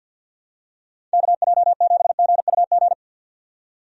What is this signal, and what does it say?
Morse code, heard as a keyed tone, about 47 wpm, 700 Hz. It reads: KJ7GUG
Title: USA Call Signs - 6 Characters - 50wpm